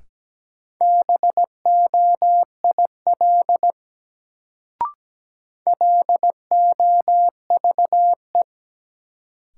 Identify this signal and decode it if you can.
Morse code, heard as a keyed tone, about 17 wpm, 700 Hz. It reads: BOIL LOVE
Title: Sets of 5 Words Encore - New Word Drill - 17wpm